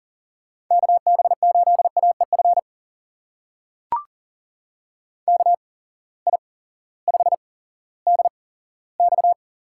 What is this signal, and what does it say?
Morse code, heard as a keyed tone, about 39 wpm, spaced as for 18 wpm, 700 Hz. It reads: KB8AEF KI5DX
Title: USA Call Signs - 6 Characters - 8x Character Spacing - 40wpm